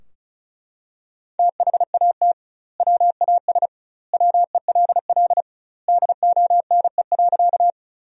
Morse code, heard as a keyed tone, about 34 wpm, 700 Hz. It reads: THAT WAS WELL DONE.
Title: 2x Word-Spacing - Sentences from Top 500 Words - 35wpm